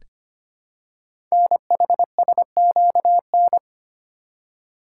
Morse code, heard as a keyed tone, about 25 wpm, 700 Hz. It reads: NHSQN
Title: Code Groups - 5 Letter:Numbers - 25wpm